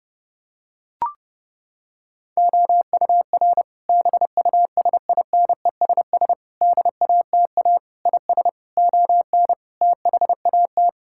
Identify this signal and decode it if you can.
Morse code, heard as a keyed tone, about 30 wpm, 700 Hz. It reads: OUR BUSINESS DATA IS ON THAT
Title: Sentences from Top 1000 Words - Review Part 6 - 30wpm